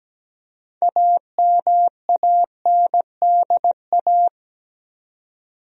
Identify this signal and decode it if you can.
Morse code, heard as a keyed tone, about 17 wpm, 700 Hz. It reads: AMANDA
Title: USA Common Names - 17wpm